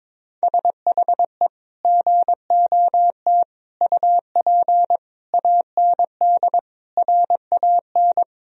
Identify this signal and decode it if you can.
Morse code, heard as a keyed tone, about 21 wpm, 700 Hz. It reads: SHE GOT UP AND RAN
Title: Sentences from Top 1000 Words - Review Part 3 - 22wpm